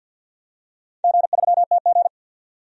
Morse code, heard as a keyed tone, about 53 wpm, 700 Hz. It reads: G3TG
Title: DXCC-100 Call Signs - 4 Characters - 50wpm